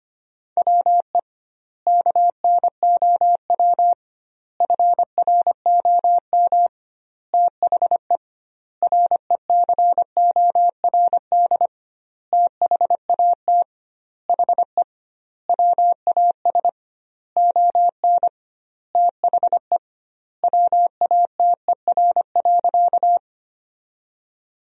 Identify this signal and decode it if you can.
Morse code, heard as a keyed tone, about 25 wpm, 700 Hz. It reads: WE KNOW FROM THE RECORD THAT HE WAS ON THE WATER.
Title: 2x Word-Spacing - Sentences from Top 500 Words - 25wpm